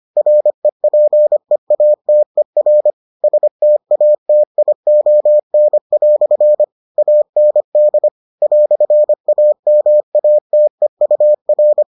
Morse code, heard as a keyed tone, about 25 wpm, 600 Hz. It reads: REPEATER STATION" AND "AMATEUR